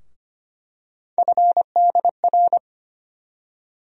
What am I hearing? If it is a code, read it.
Morse code, heard as a keyed tone, about 25 wpm, 700 Hz. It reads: FDR
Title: USA Call Sign Postfixes - 3 Letters 25wpm